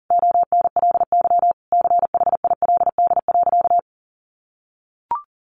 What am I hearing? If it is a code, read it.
Morse code, heard as a keyed tone, about 40 wpm, 700 Hz. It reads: ONLY CHILD.
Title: Sentences from Top 800 Words 40wpm